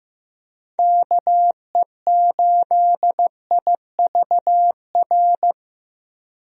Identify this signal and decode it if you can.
Morse code, heard as a keyed tone, about 16 wpm, 700 Hz. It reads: KE8IVR
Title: USA Call Signs - 6 Characters - Repeated 2x - 15wpm